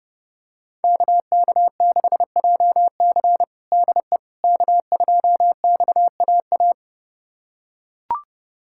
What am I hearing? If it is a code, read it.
Morse code, heard as a keyed tone, about 32 wpm, 700 Hz. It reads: KK6JC DE K2XAA
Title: USA Callsign DE Callsign - 5 Characters - 30wpm